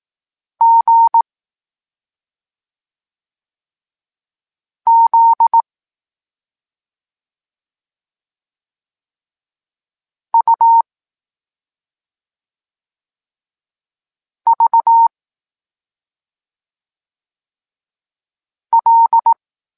Morse code, heard as a keyed tone, about 18 wpm, 920 Hz. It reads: G Z U V L
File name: Tekst-26 1